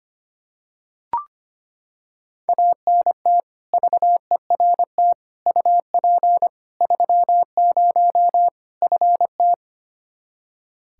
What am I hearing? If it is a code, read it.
Morse code, heard as a keyed tone, about 25 wpm, 700 Hz. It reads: ANT VERT UP 30 FT
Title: K1USN SST QSO Practice 25wpm